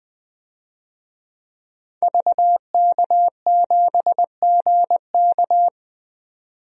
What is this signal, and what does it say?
Morse code, heard as a keyed tone, about 20 wpm, 700 Hz. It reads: VK7GK
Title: DXCC-100 Call Signs - 5 Characters - 20wpm